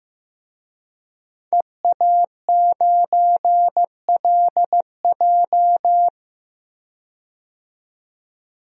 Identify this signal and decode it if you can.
Morse code, heard as a keyed tone, about 15 wpm, 700 Hz. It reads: EA9LJ
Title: DXCC-100 Call Signs - 5 Characters - 15wpm